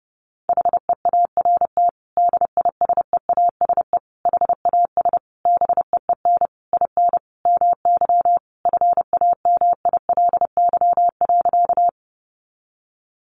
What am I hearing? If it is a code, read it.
Morse code, heard as a keyed tone, about 28 wpm, 700 Hz. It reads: HEART DISEASE HAS BEEN IN MY FAMILY.
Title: Sentences from Top 1200 Words 30wpm